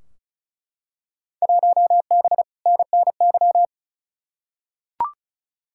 Morse code, heard as a keyed tone, about 36 wpm, 700 Hz. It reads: 1B NNY
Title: Field Day Exchanges - 35wpm